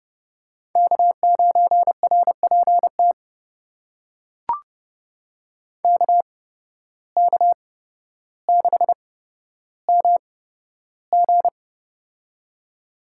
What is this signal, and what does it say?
Morse code, heard as a keyed tone, about 30 wpm, spaced as for 13 wpm, 700 Hz. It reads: K9RPT KK6MG
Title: USA Call Signs - 5 Characters - 8x Character Spacing - 30wpm